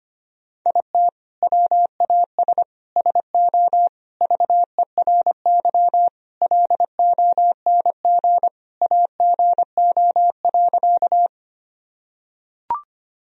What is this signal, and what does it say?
Morse code, heard as a keyed tone, about 25 wpm, 700 Hz. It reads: IT WAS SO VERY LONG AGO.
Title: Sentences from Top 1000 Words - Review Part 3 - 25wpm